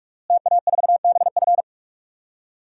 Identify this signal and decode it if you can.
Morse code, heard as a keyed tone, about 46 wpm, 700 Hz. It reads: TA4BF
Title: DXCC-100 Call Signs - 5 Characters - Repeated 2x - 45wpm